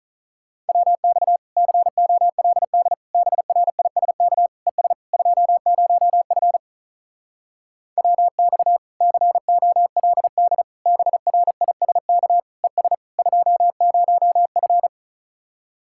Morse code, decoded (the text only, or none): WX COLD BRISK ES 20F WX COLD BRISK ES 20F